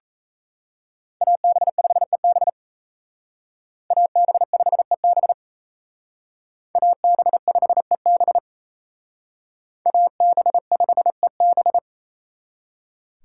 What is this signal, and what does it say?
Morse code, heard as a keyed tone, about 32 wpm, 700 Hz. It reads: AB5EB AB5EB AB5EB AB5EB
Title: Speed Racing - USA 2x2 Call Signs 28wpm